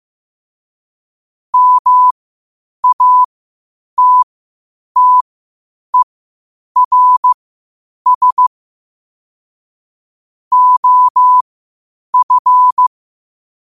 Morse code, heard as a keyed tone, about 15 wpm, spaced as for 8 wpm, 1000 Hz. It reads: MATTERS OF
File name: qotdfiles / 8wpm